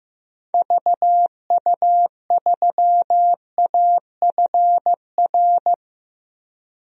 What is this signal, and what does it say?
Morse code, heard as a keyed tone, about 16 wpm, 700 Hz. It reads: VU3AFR
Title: DXCC-100 Call Signs - 6 Characters - Repeated 3x - 15wpm